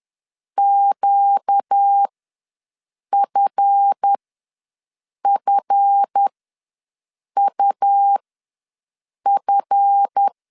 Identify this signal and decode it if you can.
Morse code, heard as a keyed tone, about 10 wpm, spaced as for 6 wpm, 800 Hz. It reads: QFFUF
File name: LEZIONI / 7 GF9Z / 7 GF ws 10wpm 3918